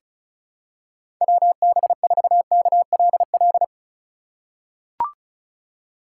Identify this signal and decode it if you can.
Morse code, heard as a keyed tone, about 36 wpm, 700 Hz. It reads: WB4KLL
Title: USA Call Signs - 6 Characters - 3x Character Spacing - 35wpm